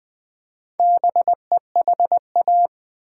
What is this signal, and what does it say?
Morse code, heard as a keyed tone, about 20 wpm, 700 Hz. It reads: BEHA
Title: Build A Word - 2 Letters At A Time - 6-Letter Words - 20wpm